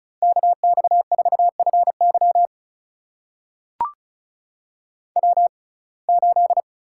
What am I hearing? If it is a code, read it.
Morse code, heard as a keyed tone, about 34 wpm, spaced as for 25 wpm, 700 Hz. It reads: KX4FY W8
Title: USA Call Signs - 5 Characters - 6x Character Spacing - 35wpm